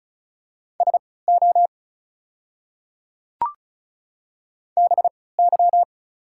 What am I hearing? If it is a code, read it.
Morse code, heard as a keyed tone, about 36 wpm, spaced as for 20 wpm, 700 Hz. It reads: SO BY
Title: 3x Character Spacing - Top 200 Words - 35wpm